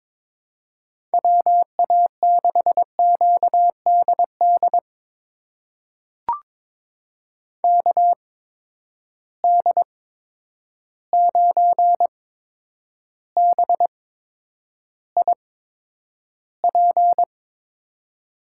Morse code, heard as a keyed tone, about 21 wpm, spaced as for 9 wpm, 700 Hz. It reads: WA6QDD KD9BIP
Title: USA Call Signs - 6 Characters - 8x Character Spacing - 22wpm